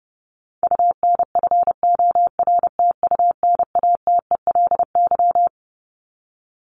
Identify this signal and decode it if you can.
Morse code, heard as a keyed tone, about 28 wpm, 700 Hz. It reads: UNFORTUNATELY
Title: Top 1751-2000 Words 30wpm